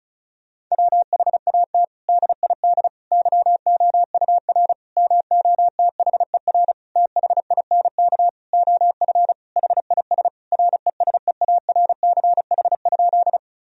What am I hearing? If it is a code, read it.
Morse code, heard as a keyed tone, about 34 wpm, 700 Hz. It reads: WHAT DID YOUR MOTHER THINK OF HIS RESEARCH?
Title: Sentences from Top 1000 Words - Review Part 4 - 35wpm